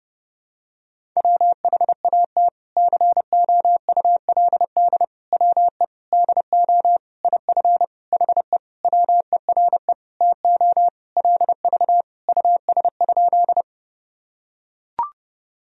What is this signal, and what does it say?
Morse code, heard as a keyed tone, about 30 wpm, 700 Hz. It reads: WHAT COULD WE DO IF HE WERE TO LV US?
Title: Sentences from Top 400 Words and Abbreviations 30wpm